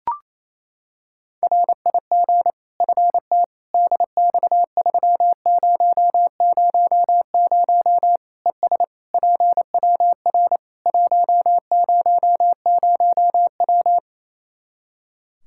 Morse code, heard as a keyed tone, about 28 wpm, 700 Hz. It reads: RIG FT DX3000 ES PWR 100W
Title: Practice QSO - Exchange 2 - Info - 28wpm